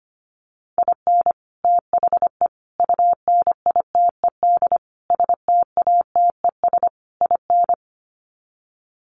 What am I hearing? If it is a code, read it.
Morse code, heard as a keyed tone, about 25 wpm, 700 Hz. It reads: IN THE UNITED STATES IN
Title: Sets of 5 Words 25wpm